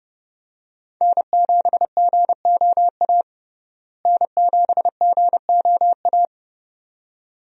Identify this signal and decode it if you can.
Morse code, heard as a keyed tone, about 29 wpm, 700 Hz. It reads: N7GOA N7GOA
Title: USA Call Signs - 5 Characters - Repeated 2x - 30wpm